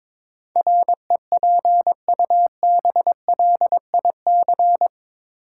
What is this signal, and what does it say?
Morse code, heard as a keyed tone, about 22 wpm, 700 Hz. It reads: REPUBLIC
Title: Sentences from Top 1500 Words - New Word Drill - 22wpm